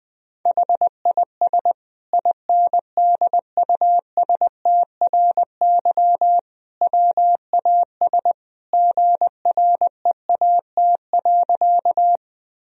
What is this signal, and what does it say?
Morse code, heard as a keyed tone, about 21 wpm, 700 Hz. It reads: HIS INDUSTRY WAS GREAT.
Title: Sentences from Top 1000 Words - Review Part 4 - 20wpm